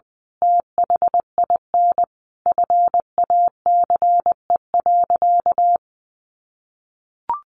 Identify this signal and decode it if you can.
Morse code, heard as a keyed tone, about 20 wpm, 700 Hz. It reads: THIN FACE.
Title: A Sound Of Thunder 20wpm